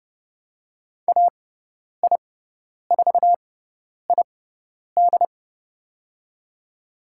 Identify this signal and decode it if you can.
Morse code, heard as a keyed tone, about 30 wpm, spaced as for 10 wpm, 700 Hz. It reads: AI4ID
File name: Lesson 16g - Letter c - Call Signs 30wpm